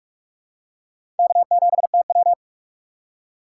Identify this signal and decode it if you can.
Morse code, heard as a keyed tone, about 45 wpm, 700 Hz. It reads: K7TW
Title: USA Call Signs - 4 Characters - 4x Character Spacing - 45wpm